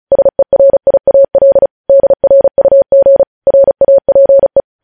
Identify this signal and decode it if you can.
Morse code, heard as a keyed tone, about 36 wpm, 550 Hz. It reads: SERIAL DRUG RAPE
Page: News Headlines in Morse Code at 35 WPM